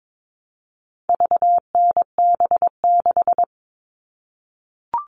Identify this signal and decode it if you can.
Morse code, heard as a keyed tone, about 22 wpm, 700 Hz. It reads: VNB6
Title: All Cut Numbers - 4-Digit 22wpm